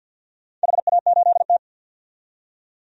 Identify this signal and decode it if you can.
Morse code, heard as a keyed tone, about 49 wpm, 700 Hz. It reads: HA8T
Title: DXCC-100 Call Signs - 4 Characters - 50wpm